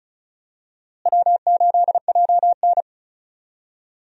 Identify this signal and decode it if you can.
Morse code, heard as a keyed tone, about 34 wpm, 700 Hz. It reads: W8JN
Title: USA Call Signs - 4 Characters - Repeated 2x - 35wpm